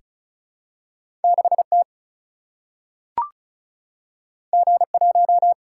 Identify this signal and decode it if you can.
Morse code, heard as a keyed tone, about 36 wpm, 700 Hz. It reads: 6 T G 1
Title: Single-Letter Single-Number 35wpm